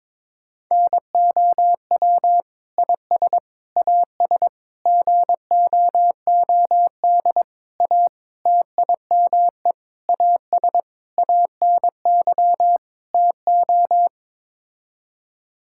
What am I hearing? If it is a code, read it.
Morse code, heard as a keyed tone, about 22 wpm, 700 Hz. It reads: NOW IS AS GOOD A TIME AS ANY TO
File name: Sets of 9 Words - 22wpm